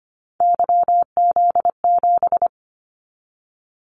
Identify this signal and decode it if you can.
Morse code, heard as a keyed tone, about 25 wpm, 700 Hz. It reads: YZ7
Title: Callsign Prefixes for Top 25 DXCC - 25wpm